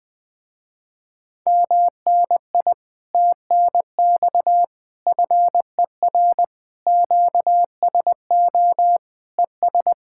Morse code, heard as a keyed tone, about 20 wpm, 700 Hz. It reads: MNI TNX FER QSO ES
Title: Speed-Racing Common QSO Phrases - 17wpm